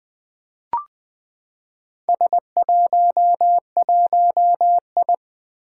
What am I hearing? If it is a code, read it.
Morse code, heard as a keyed tone, about 20 wpm, 700 Hz. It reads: S11I
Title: Four Challenging Letter-Number Combinations 20wpm